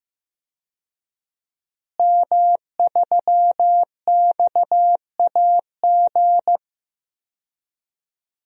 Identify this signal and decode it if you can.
Morse code, heard as a keyed tone, about 15 wpm, 700 Hz. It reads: M3XAG
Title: DXCC-100 Call Signs - 5 Characters - Repeated 3x - 15wpm